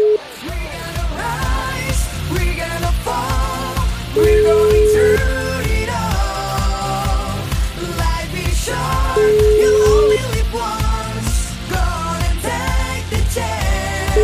Ringback tone (Europe, 425 Hz), with noise.